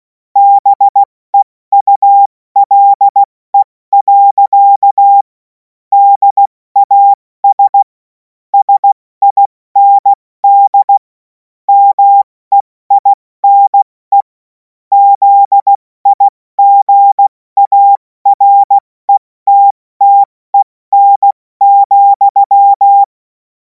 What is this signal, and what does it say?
Morse code, heard as a keyed tone, about 16 wpm, 800 Hz. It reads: BEULE. DAS SIND MEINE ZIGARETTEN,
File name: Morsekurs / Witze / Witz 10. März 2026 70 F80